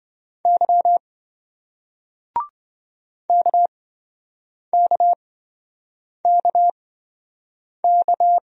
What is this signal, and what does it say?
Morse code, heard as a keyed tone, about 27 wpm, spaced as for 7 wpm, 700 Hz. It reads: Y KKKK